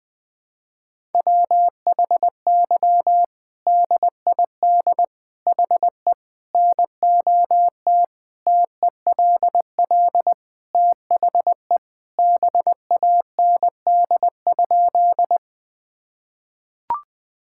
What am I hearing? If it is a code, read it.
Morse code, heard as a keyed tone, about 20 wpm, 700 Hz. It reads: WHY DID HE NOT TELL THE BAND?
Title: Sentences from Top 1600 Words 20wpm